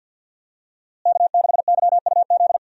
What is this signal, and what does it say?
Morse code, heard as a keyed tone, about 52 wpm, 700 Hz. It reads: K6YUZ
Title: USA Call Signs - 5 Characters - Repeated 2x - 50wpm